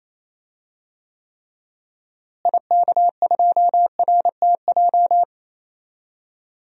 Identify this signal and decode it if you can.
Morse code, heard as a keyed tone, about 27 wpm, 700 Hz. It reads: IK2RTJ
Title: DXCC-100 Call Signs - 6 Characters - Repeated 2x - 28wpm